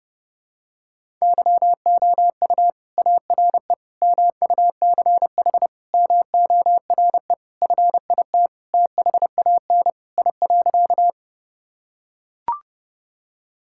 Morse code, heard as a keyed tone, about 30 wpm, 700 Hz. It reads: YOU ARE MUCH MORE FIT THAN I.
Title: Sentences from Top 1000 Words - Review Part 5 - 30wpm